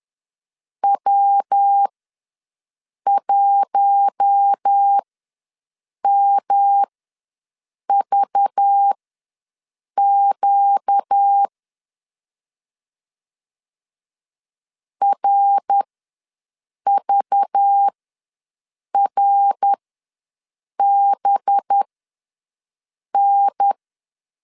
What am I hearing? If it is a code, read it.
Morse code, heard as a keyed tone, about 10 wpm, spaced as for 6 wpm, 800 Hz. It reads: W1MVQ RVRBN
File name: LEZIONI / 5 LBVC / 5 LBV ws 10wpm 3645